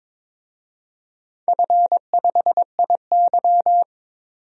Morse code, heard as a keyed tone, about 22 wpm, 700 Hz. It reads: F5IY
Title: DXCC-100 Call Signs - 4 Characters - 22wpm